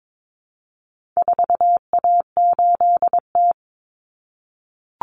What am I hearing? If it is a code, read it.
Morse code, heard as a keyed tone, about 22 wpm, 700 Hz. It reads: 4A8T
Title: Common Cut Numbers - 4-Digit 22wpm